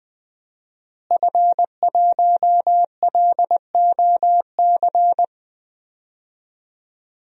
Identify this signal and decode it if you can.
Morse code, heard as a keyed tone, about 21 wpm, 700 Hz. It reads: F1LOC
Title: DXCC-100 Call Signs - 5 Characters - Repeated 3x - 20wpm